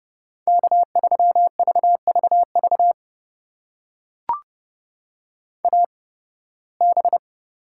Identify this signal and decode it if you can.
Morse code, heard as a keyed tone, about 32 wpm, spaced as for 20 wpm, 700 Hz. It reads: K3VVV AB
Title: USA Call Signs - 5 Characters - 8x Character Spacing - 30wpm